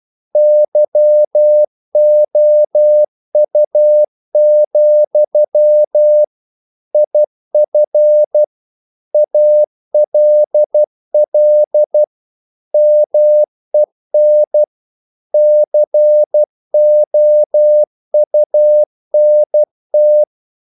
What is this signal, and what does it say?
Morse code, heard as a keyed tone, about 12 wpm, 600 Hz. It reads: YOU, IF ALL MEN COUNT
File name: CW / Practice / If12